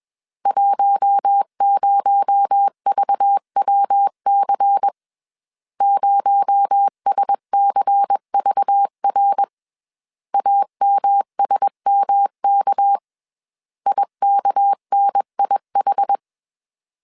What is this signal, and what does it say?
Morse code, heard as a keyed tone, about 21 wpm, 800 Hz. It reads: 10VWC 0SCVR AMSMK IKNIH
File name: CW / DET / 1 6 1500C S B 20wpm 2048